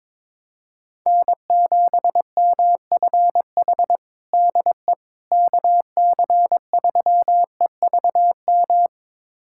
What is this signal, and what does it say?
Morse code, heard as a keyed tone, about 22 wpm, 700 Hz. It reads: N7MFH DE KC3EVM
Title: USA Callsign DE Callsign 22wpm